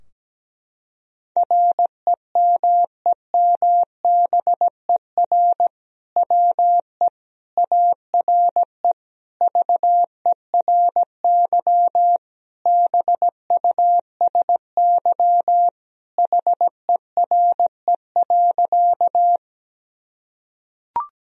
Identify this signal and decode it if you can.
Morse code, heard as a keyed tone, about 17 wpm, 700 Hz. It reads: REMEMBER WE ARE VERY BUSY HERE.
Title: Sentences from Top 1000 Words - Review Part 3 - 17wpm